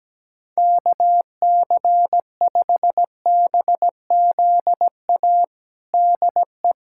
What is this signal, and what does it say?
Morse code, heard as a keyed tone, about 17 wpm, 700 Hz. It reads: KC5BZA DE
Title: USA Callsign DE Callsign 17wpm